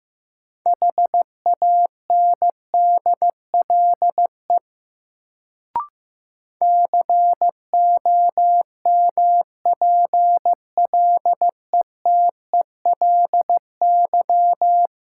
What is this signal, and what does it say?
Morse code, heard as a keyed tone, about 15 wpm, 700 Hz. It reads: HANDLE COMPLETELY
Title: Sentences from Top 1100 Words - New Word Drill Alt - 15wpm